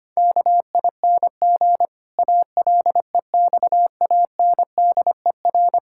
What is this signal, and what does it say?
Morse code, heard as a keyed tone, about 25 wpm, 700 Hz. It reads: KING ALEXANDER
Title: Pangram Challenge v2 - 25wpm